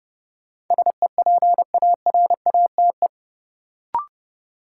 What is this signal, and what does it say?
Morse code, heard as a keyed tone, about 29 wpm, 700 Hz. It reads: SEPARATE
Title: Sentences from Top 1000 Words - Review Part 7 - New Word Drill - 30wpm